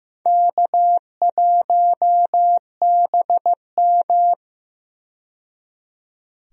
Morse code, heard as a keyed tone, about 15 wpm, 700 Hz. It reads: K1BM
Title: USA Call Signs - 4 Characters - 15wpm